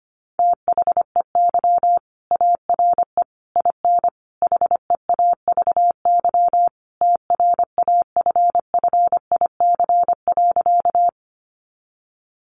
Morse code, heard as a keyed tone, about 25 wpm, 700 Hz. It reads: THEY ARE IN HEAVY TRAFFIC.
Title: Sentences from Top 1300 Words 25wpm